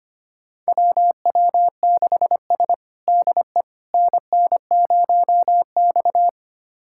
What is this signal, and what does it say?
Morse code, heard as a keyed tone, about 25 wpm, 700 Hz. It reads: WW6S DE NN0X
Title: USA Callsign DE Callsign - 4 Characters - 25wpm